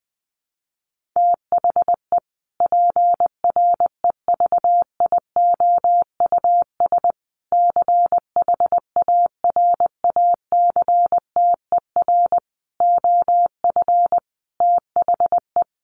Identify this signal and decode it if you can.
Morse code, heard as a keyed tone, about 20 wpm, 700 Hz. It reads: THE PREVIOUS CHARACTER OF THE
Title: Sentences from Top 1500 Words 20wpm